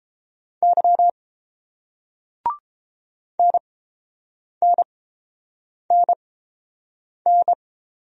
Morse code, heard as a keyed tone, about 29 wpm, spaced as for 7 wpm, 700 Hz. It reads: Y NNNN